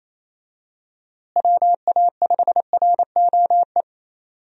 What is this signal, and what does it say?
Morse code, heard as a keyed tone, about 27 wpm, 700 Hz. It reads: WA5ROE
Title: USA Call Signs - 6 Characters - Repeated 3x - 28wpm